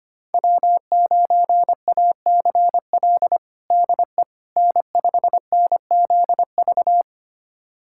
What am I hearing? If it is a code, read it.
Morse code, heard as a keyed tone, about 25 wpm, 700 Hz. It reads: W9ACL DE N5NZV
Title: USA Callsign DE Callsign - 5 Characters - 25wpm